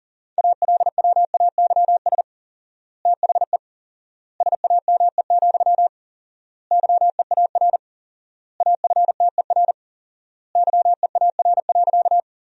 Morse code, heard as a keyed tone, about 40 wpm, 700 Hz. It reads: ALWAYS THE SAME, YEAR AFTER YEAR.
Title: 4x Word-Spacing - Sentences from Top 300 Words - 40wpm